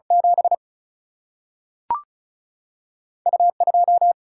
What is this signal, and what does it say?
Morse code, heard as a keyed tone, about 36 wpm, spaced as for 3 wpm, 700 Hz. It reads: 7U 2
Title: Single-Letter Single-Number 35wpm